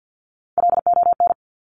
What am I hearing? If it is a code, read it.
Morse code, heard as a keyed tone, about 50 wpm, 700 Hz. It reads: LON
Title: Top Three Letter Combinations in Words 50wpm